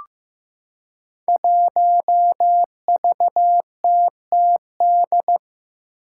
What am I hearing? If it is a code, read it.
Morse code, heard as a keyed tone, about 15 wpm, 700 Hz. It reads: 1VTTD
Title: Five Challenging Letter-Number Combinations 15wpm